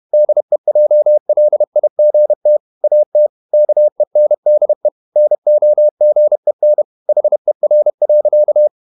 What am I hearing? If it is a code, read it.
Morse code, heard as a keyed tone, about 31 wpm, 600 Hz. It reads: DEJLIGT AT KENDE NOGEN HER.
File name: Ferie QRQ 31wpm